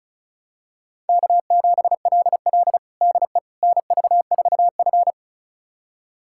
Morse code, heard as a keyed tone, about 36 wpm, 700 Hz. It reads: K7LL DE NV4F
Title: USA Callsign DE Callsign - 4 Characters - 35wpm